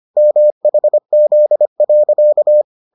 Morse code, heard as a keyed tone, about 25 wpm, 600 Hz. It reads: MHZ.